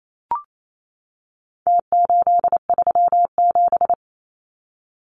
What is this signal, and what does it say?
Morse code, heard as a keyed tone, about 28 wpm, 700 Hz. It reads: T837
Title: Numbers - Contest Style - 4-Digit 28wpm